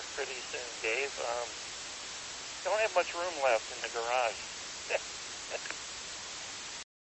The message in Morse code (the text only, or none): none